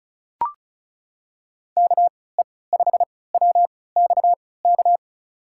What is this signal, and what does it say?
Morse code, heard as a keyed tone, about 36 wpm, spaced as for 20 wpm, 700 Hz. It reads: KE5WXK
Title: USA Call Signs - 6 Characters - 3x Character Spacing - 35wpm